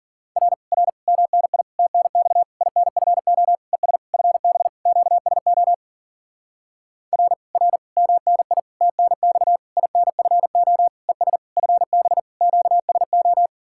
Morse code, heard as a keyed tone, about 49 wpm, 700 Hz. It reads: R R MNI TNX INFO ES FB QSO R R MNI TNX INFO ES FB QSO
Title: Speed-Racing Common QSO Phrases - 40wpm